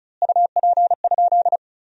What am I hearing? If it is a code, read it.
Morse code, heard as a keyed tone, about 34 wpm, 700 Hz. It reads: UP?
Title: 4x Word-Spacing - Sentences from Top 500 Words - 35wpm